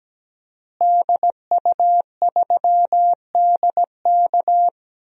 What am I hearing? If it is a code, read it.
Morse code, heard as a keyed tone, about 17 wpm, 700 Hz. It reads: DU3DK
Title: DXCC-100 Call Signs - 5 Characters - Repeated 3x - 17wpm